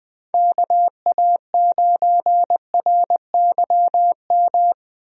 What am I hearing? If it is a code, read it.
Morse code, heard as a keyed tone, about 20 wpm, 700 Hz. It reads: KA9RYM
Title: USA Call Signs - 6 Characters - Repeated 3x - 20wpm